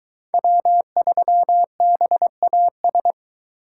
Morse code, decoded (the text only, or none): W3BAS